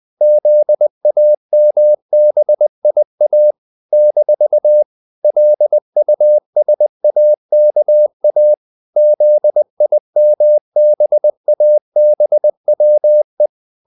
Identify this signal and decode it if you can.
Morse code, heard as a keyed tone, about 20 wpm, 600 Hz. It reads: ZAMBIA - LUSAKA ZIMBABWE
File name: CW / Practice / NationalCapitals20